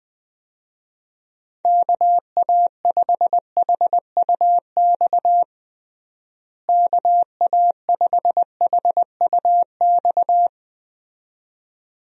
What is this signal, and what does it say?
Morse code, heard as a keyed tone, about 20 wpm, 700 Hz. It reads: KA5HUX KA5HUX
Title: USA Call Signs - 6 Characters - Repeated 2x - 20wpm